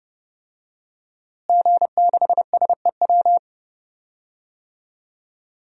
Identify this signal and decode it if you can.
Morse code, heard as a keyed tone, about 30 wpm, 700 Hz. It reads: G6SEW
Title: DXCC-100 Call Signs - 5 Characters - Repeated 2x - 30wpm